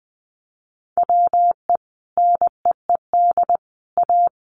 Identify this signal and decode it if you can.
Morse code, heard as a keyed tone, about 19 wpm, 700 Hz. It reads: WE NEED A
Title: Sentences from Top 1500 Words 20wpm